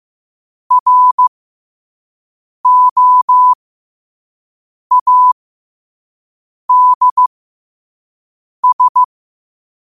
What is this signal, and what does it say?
Morse code, heard as a keyed tone, about 15 wpm, spaced as for 5 wpm, 1000 Hz. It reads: ROADS
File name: qotdfiles / 5wpm